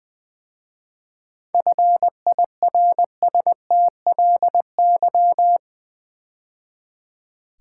Morse code, heard as a keyed tone, about 20 wpm, 700 Hz. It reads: FIRSTLY
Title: 7 Letter Words - First 3 Letters Given - 20wpm